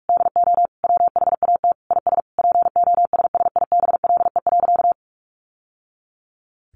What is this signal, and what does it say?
Morse code, heard as a keyed tone, about 44 wpm, 700 Hz. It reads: DO WHAT IS POSSIBLE.
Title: Sentences from Top 900 Words 45wpm